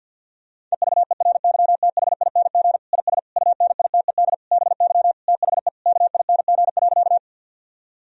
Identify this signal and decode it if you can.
Morse code, heard as a keyed tone, about 50 wpm, 700 Hz. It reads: EVERYTHING IS UNITED BY THE KING.
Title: Sentences from Top 1000 Words - Review Part 4 - 50wpm